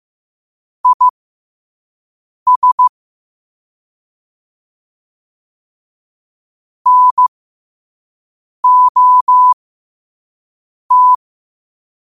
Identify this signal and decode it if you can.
Morse code, heard as a keyed tone, about 15 wpm, spaced as for 5 wpm, 1000 Hz. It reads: IS NOT
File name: qotdfiles / 5wpm